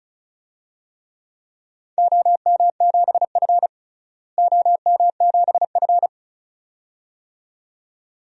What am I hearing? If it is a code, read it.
Morse code, heard as a keyed tone, about 34 wpm, 700 Hz. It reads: OM7F OM7F
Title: DXCC-100 Call Signs - 4 Characters - Repeated 2x - 35wpm